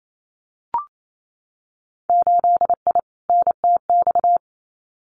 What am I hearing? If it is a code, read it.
Morse code, heard as a keyed tone, about 28 wpm, 690 Hz. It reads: 8I NTX
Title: Winter Field Day Exchanges Encore 28wpm